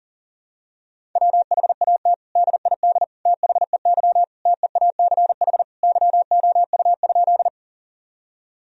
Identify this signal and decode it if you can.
Morse code, heard as a keyed tone, about 40 wpm, 700 Hz. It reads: WHAT DID THEY TEACH YOU?
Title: Sentences from Top 1000 Words - Review Part 2 - 40wpm